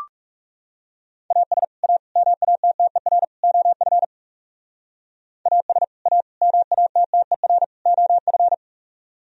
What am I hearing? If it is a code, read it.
Morse code, heard as a keyed tone, about 35 wpm, 700 Hz. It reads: AS A MATTER OF AS A MATTER OF
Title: Speed-Racing - Sets of 4 Words 30wpm